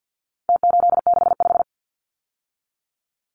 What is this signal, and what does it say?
Morse code, heard as a keyed tone, about 49 wpm, 700 Hz. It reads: T865
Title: Numbers - Contest Style - 4-Digit 50wpm